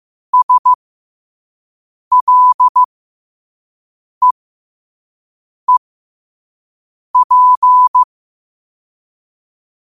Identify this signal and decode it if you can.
Morse code, heard as a keyed tone, about 15 wpm, spaced as for 5 wpm, 1000 Hz. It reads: SLEEP
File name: qotdfiles / 5wpm